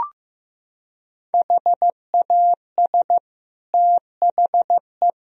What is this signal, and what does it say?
Morse code, heard as a keyed tone, about 15 wpm, 700 Hz. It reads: HAS THE
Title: Sentences from Top 1000 Words - Review Part 4 - 15wpm